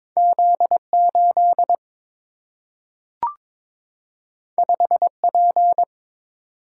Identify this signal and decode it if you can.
Morse code, heard as a keyed tone, about 22 wpm, 700 Hz. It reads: Z8 5P
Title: Single-Letter Single-Number 22wpm